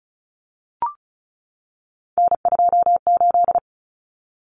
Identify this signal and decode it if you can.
Morse code, heard as a keyed tone, about 34 wpm, 700 Hz. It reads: N28
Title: Common Cut Numbers - 3-Digit 35wpm